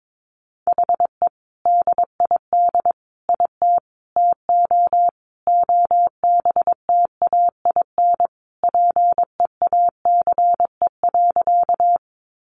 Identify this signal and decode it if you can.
Morse code, heard as a keyed tone, about 22 wpm, 700 Hz. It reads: HE DID IT TO OBTAIN PEACE.
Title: Sentences from Top 1500 Words 22wpm